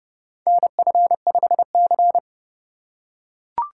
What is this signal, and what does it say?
Morse code, heard as a keyed tone, about 29 wpm, 700 Hz. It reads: NF5C